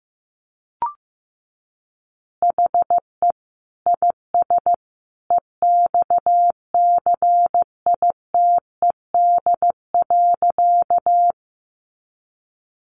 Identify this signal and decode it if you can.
Morse code, heard as a keyed tone, about 15 wpm, 700 Hz. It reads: HE IS EXCITED.
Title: Additional Easy Phrases 15wpm